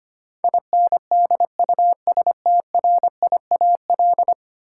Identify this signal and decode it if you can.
Morse code, heard as a keyed tone, about 25 wpm, 700 Hz. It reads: INDUSTRIAL
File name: Ten Letter Words - 25wpm